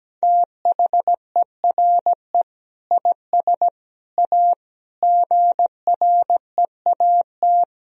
Morse code, heard as a keyed tone, about 17 wpm, 700 Hz. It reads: THERE IS A GREAT
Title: Sentences from Top 1000 Words - Review Part 4 - 17wpm